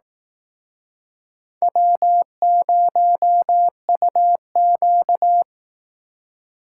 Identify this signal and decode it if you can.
Morse code, heard as a keyed tone, about 19 wpm, 700 Hz. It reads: W0UQ